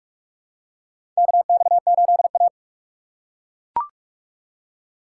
Morse code, heard as a keyed tone, about 43 wpm, 700 Hz. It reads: KX8A
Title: USA Call Signs - 4 Characters - 6x Character Spacing - 45wpm